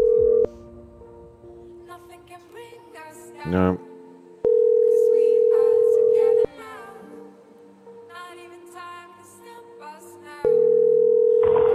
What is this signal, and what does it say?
Ringback tone (North America), 440 + 480 Hz.